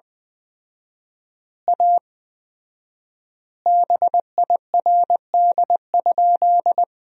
Morse code, heard as a keyed tone, about 20 wpm, 700 Hz. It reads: A BIRD?
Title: 4x Word-Spacing - Sentences from Top 400 Words - 20wpm